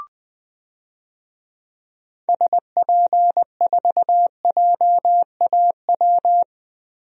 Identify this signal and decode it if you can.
Morse code, heard as a keyed tone, about 21 wpm, 700 Hz. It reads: SP4JAW
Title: DXCC-100 Call Signs - 6 Characters - 20wpm